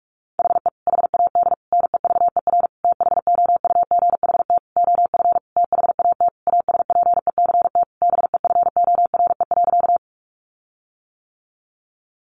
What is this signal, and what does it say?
Morse code, heard as a keyed tone, about 44 wpm, 700 Hz. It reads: HE HAD NEVER THOUGHT OF THAT ASPECT BEFORE.
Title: Sentences from Top 1300 Words 45wpm